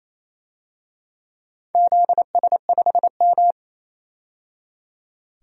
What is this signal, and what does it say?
Morse code, heard as a keyed tone, about 28 wpm, 700 Hz. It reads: ZS5M
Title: DXCC-100 Call Signs - 4 Characters - 28wpm